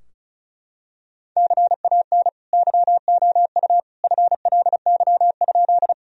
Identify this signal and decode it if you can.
Morse code, heard as a keyed tone, about 36 wpm, 700 Hz. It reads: CAN YOU FLY?
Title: Sentences from Top 1000 Words - Review Part 2 - 35wpm